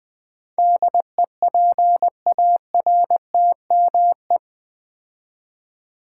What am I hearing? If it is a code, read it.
Morse code, heard as a keyed tone, about 20 wpm, 700 Hz. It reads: DEPARTME
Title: Build A Word - 4 Letters At A Time - 10-Letter Words - 20wpm